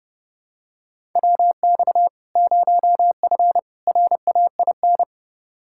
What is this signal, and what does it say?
Morse code, heard as a keyed tone, about 30 wpm, 700 Hz. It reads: WX 0F RAIN
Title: Practice QSO - Exchange 2 - Info - 30wpm